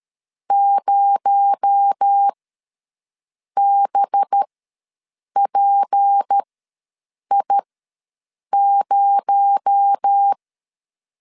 Morse code, heard as a keyed tone, about 13 wpm, spaced as for 6 wpm, 800 Hz. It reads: 0BPI0
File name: LEZIONI / 6 PKX0 / 6 PKX0 ws 12wpm 3448